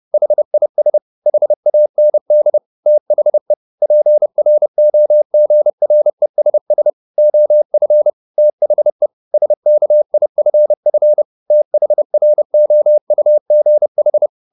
Morse code, heard as a keyed tone, about 30 wpm, 600 Hz. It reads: HIS HAND THE PROGRESS OF THE SKIFF THROUGH
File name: the old man and the sea / the old man and the sea 1 30wpm